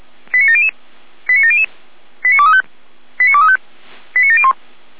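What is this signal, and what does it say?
Five-tone selective call (ZVEI), heard as quick runs of five tones, tones of about 70 ms. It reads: 89700 89700 89235 89235 89701
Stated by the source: ZVEI five-tone selective call received by SDR, fire brigade Feuerwehr Orlamünde.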